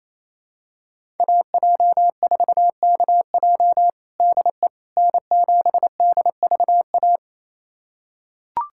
Morse code, heard as a keyed tone, about 28 wpm, 700 Hz. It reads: AJ4KJ DE N7DVA
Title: USA Callsign DE Callsign - 5 Characters - 28wpm